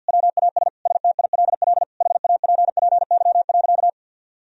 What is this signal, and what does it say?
Morse code, heard as a keyed tone, about 50 wpm, 700 Hz. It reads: WAS STILL HAPPY.
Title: Sentences from Top 1000 Words - Review Part 3 - 50wpm